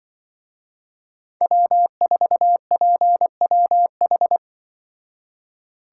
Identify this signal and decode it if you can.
Morse code, heard as a keyed tone, about 25 wpm, 700 Hz. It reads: W4PWH
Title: Speed Racing - USA 1x3 Call Signs 20wpm